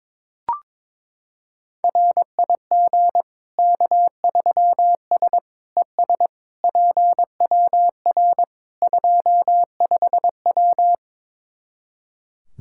Morse code, heard as a keyed tone, about 22 wpm, 700 Hz. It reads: RIG K3S ES PWR 25W
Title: Practice QSO - Exchange 2 - Info - 22wpm